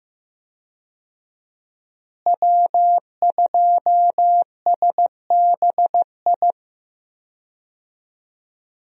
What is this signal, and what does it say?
Morse code, heard as a keyed tone, about 15 wpm, 700 Hz. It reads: W2SBI
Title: USA Call Signs - 5 Characters - Repeated 3x - 15wpm